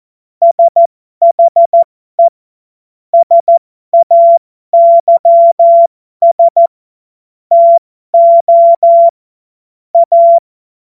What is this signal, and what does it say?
Morse code, heard as a keyed tone, about 14 wpm, spaced as for 12 wpm, 680 Hz. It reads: SHE SAYS TO A
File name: Morsekurs / Witze